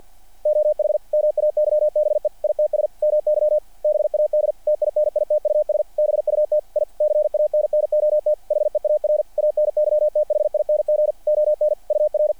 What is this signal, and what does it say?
Morse code, heard as a keyed tone, about 48 wpm, 600 Hz, with radio noise. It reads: OH MAYBE ITS MY BAD TINITUS BUT I CANNOT HEAR ANYTHING ON UR
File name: archived / CRSnet